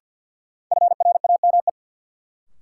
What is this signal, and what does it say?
Morse code, heard as a keyed tone, about 51 wpm, 700 Hz. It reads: FRAME
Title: Sentences from Top 1000 Words - Review Part 6 - New Word Drill - 50wpm